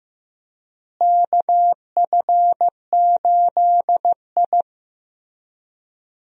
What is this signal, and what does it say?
Morse code, heard as a keyed tone, about 15 wpm, 700 Hz. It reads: KF8I
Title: USA Call Signs - 4 Characters - Repeated 2x - 15wpm